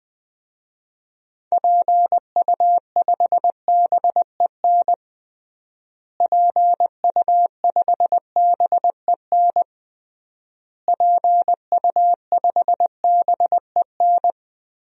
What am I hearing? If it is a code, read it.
Morse code, heard as a keyed tone, about 20 wpm, 700 Hz. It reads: PU5BEN PU5BEN PU5BEN
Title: DXCC-100 Call Signs - 6 Characters - Repeated 3x - 20wpm